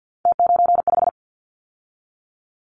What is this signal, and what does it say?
Morse code, heard as a keyed tone, about 50 wpm, 700 Hz. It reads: T95
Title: Callsign Prefixes for Top 50 DXCC - 50wpm